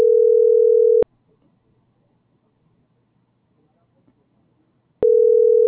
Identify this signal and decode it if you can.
Ringback tone (North America), 440 + 480 Hz.